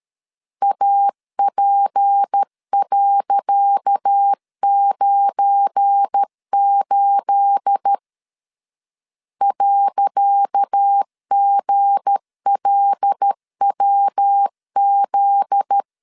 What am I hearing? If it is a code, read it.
Morse code, heard as a keyed tone, about 12 wpm, 800 Hz. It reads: AP.98 .GLWZ
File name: LEZIONI / 10 SIGNS / 10C DOT 12wpm 2451